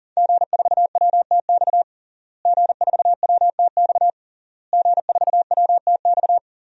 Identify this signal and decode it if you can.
Morse code, heard as a keyed tone, about 40 wpm, 700 Hz. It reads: G4WTX G4WTX G4WTX
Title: DXCC-100 Call Signs - 5 Characters - Repeated 3x - 40wpm